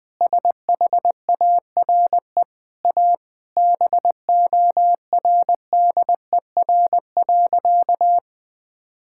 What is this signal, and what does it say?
Morse code, heard as a keyed tone, about 20 wpm, 700 Hz. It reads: SHARE A BORDER.
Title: Sentences with 5-Word Phrases - 20wpm